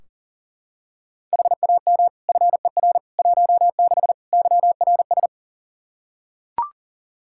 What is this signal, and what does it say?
Morse code, heard as a keyed tone, about 40 wpm, 700 Hz. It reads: HAM FER 16 YRS
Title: Practice QSO - Exchange 3 - More Info - 40wpm